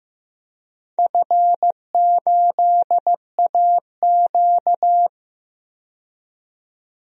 Morse code, heard as a keyed tone, about 15 wpm, 700 Hz. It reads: F8AQ
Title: DXCC-100 Call Signs - 4 Characters - 15wpm